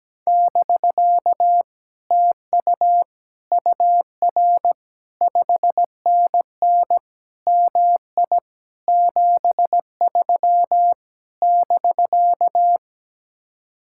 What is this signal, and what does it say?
Morse code, heard as a keyed tone, about 17 wpm, 700 Hz. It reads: <BK> TU UR 5NN MI 73 <BK>
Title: Speed-Racing Common QSO Phrases Encore - 17wpm